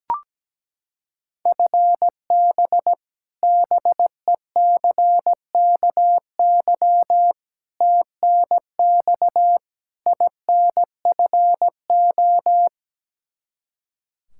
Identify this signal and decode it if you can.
Morse code, heard as a keyed tone, about 17 wpm, 700 Hz. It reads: FB BECKY TNX INFO
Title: Practice QSO - Exchange 3 - More Info - 17wpm